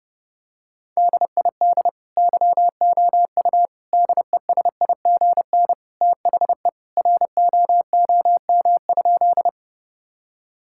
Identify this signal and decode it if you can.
Morse code, heard as a keyed tone, about 30 wpm, 700 Hz. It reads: DID YOU DESIGN THE ROOM?
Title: Sentences from Top 1000 Words - Review Part 5 - 30wpm